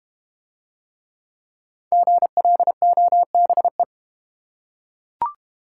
Morse code, heard as a keyed tone, about 31 wpm, 700 Hz. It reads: GLOBE
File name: Easy Words - Speed-Racing with Hints - 25wpm